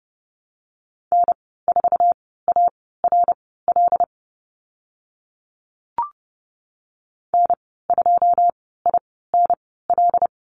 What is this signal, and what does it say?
Morse code, heard as a keyed tone, about 30 wpm, spaced as for 17 wpm, 700 Hz. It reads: N4ARL N2INL
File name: Lesson 14g - Number 2 - Call Signs 30wpm